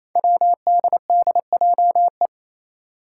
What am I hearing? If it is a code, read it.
Morse code, heard as a keyed tone, about 28 wpm, 700 Hz. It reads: WDDJE
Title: Five Challenging Letter-Number Combinations 28wpm